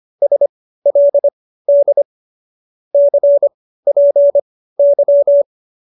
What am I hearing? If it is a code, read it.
Morse code, heard as a keyed tone, about 25 wpm, spaced as for 15 wpm, 590 Hz. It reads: SLD CPY